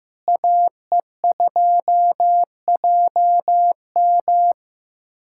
Morse code, heard as a keyed tone, about 15 wpm, 700 Hz. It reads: AE2JM
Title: USA Callsign DE Callsign 15wpm